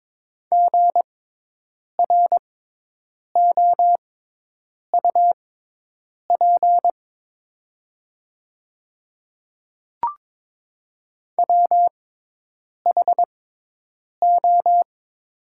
Morse code, heard as a keyed tone, about 22 wpm, spaced as for 8 wpm, 700 Hz. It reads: GROUP WHO
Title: 6x Character Spacing - Top 500 Words - 22wpm